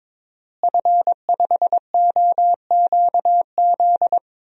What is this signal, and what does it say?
Morse code, heard as a keyed tone, about 22 wpm, 700 Hz. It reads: F5OQZ
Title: DXCC-100 Call Signs - 5 Characters - Repeated 2x - 22wpm